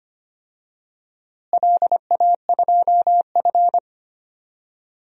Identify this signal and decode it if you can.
Morse code, heard as a keyed tone, about 25 wpm, 700 Hz. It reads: LA2F
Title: DXCC-100 Call Signs - 4 Characters - Repeated 2x - 25wpm